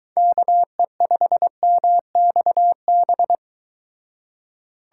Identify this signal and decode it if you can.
Morse code, heard as a keyed tone, about 23 wpm, 700 Hz. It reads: KE5MXB